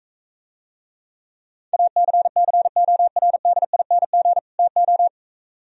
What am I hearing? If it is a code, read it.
Morse code, heard as a keyed tone, about 44 wpm, 700 Hz. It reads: ACCORDING TO